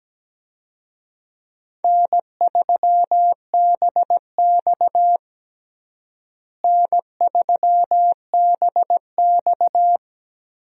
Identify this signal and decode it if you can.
Morse code, heard as a keyed tone, about 17 wpm, 700 Hz. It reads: N3BX N3BX
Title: USA Call Signs - 4 Characters - Repeated 2x - 17wpm